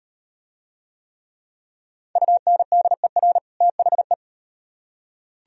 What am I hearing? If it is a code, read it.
Morse code, heard as a keyed tone, about 36 wpm, 700 Hz. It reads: UNDER THE